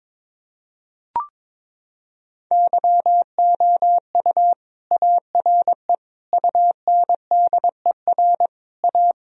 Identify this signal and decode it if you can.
Morse code, heard as a keyed tone, about 22 wpm, 700 Hz. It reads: YOU ARE UNDER A
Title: Sentences from Top 1000 Words - Review Part 1 - 22wpm